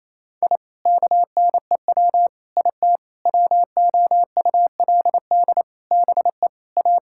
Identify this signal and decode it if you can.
Morse code, heard as a keyed tone, about 28 wpm, 700 Hz. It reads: I KNEW IT WOULD BE A
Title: Sentences from Top 1000 Words - Review Part 6 - 28wpm